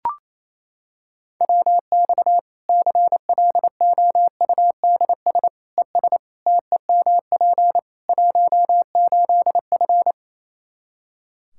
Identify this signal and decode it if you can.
Morse code, heard as a keyed tone, about 28 wpm, 700 Hz. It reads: WX CLOUDS ES TEMP 18F
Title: Practice QSO - Exchange 2 - Info - 28wpm